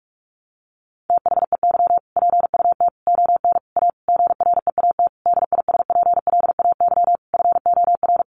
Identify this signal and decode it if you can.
Morse code, heard as a keyed tone, about 46 wpm, 700 Hz. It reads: THEY PUT ON A GREAT DISPLAY FOR
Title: Sentences from Top 1200 Words 45wpm